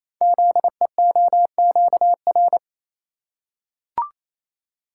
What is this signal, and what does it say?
Morse code, heard as a keyed tone, about 28 wpm, 700 Hz. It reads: ZEOQR
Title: Code Groups - 5 Letters - 28wpm